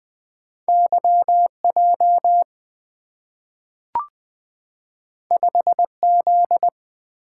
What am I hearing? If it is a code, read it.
Morse code, heard as a keyed tone, about 20 wpm, 690 Hz. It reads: YJ 5Z
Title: Two Letter-Number Combinations - 20wpm